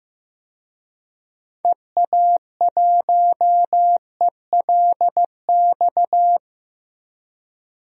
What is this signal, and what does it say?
Morse code, heard as a keyed tone, about 15 wpm, 700 Hz. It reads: EA1ELX
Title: DXCC-100 Call Signs - 6 Characters - Repeated 2x - 15wpm